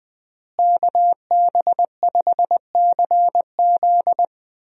Morse code, heard as a keyed tone, about 20 wpm, 700 Hz. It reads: KB5CZ
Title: USA Call Signs - 5 Characters - Repeated 2x - 20wpm